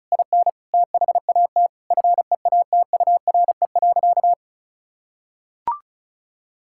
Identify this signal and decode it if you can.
Morse code, heard as a keyed tone, about 36 wpm, 700 Hz. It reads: IN THAT FEATURE.
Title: Sentences from Top 1000 Words - Review Part 5 - 35wpm